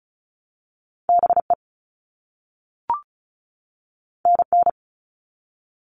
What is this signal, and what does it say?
Morse code, heard as a keyed tone, about 36 wpm, 700 Hz. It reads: BE NN
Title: All Cut Numbers - 2-Digit 35wpm